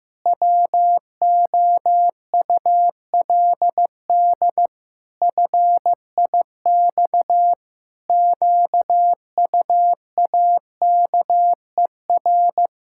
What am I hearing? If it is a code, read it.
Morse code, heard as a keyed tone, about 15 wpm, 700 Hz. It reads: WOULD FIX QUAKER
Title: Pangram Challenge v2 - 15wpm